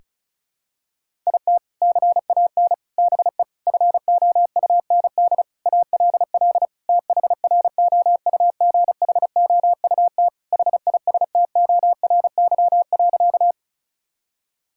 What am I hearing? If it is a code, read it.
Morse code, heard as a keyed tone, about 36 wpm, 700 Hz. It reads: IT CAN BE FOUND ALL THROUGHOUT HISTORY.
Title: Sentences from Top 1000 Words - Review Part 5 - 35wpm